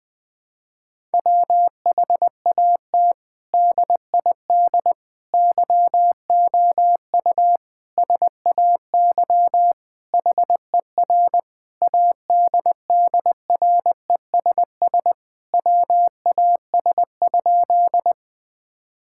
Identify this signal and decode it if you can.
Morse code, heard as a keyed tone, about 21 wpm, 700 Hz. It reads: WHAT DID YOU SAY HER ADDRESS WAS?
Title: Sentences from Top 1000 Words - Review Part 6 - 20wpm